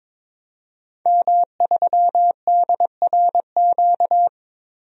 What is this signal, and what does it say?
Morse code, heard as a keyed tone, about 22 wpm, 700 Hz. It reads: M3DRQ
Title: DXCC-100 Call Signs - 5 Characters - Repeated 2x - 22wpm